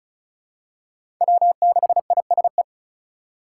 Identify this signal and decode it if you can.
Morse code, heard as a keyed tone, about 36 wpm, 700 Hz. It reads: W6ISE
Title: USA Call Signs - 5 Characters - 4x Character Spacing - 35wpm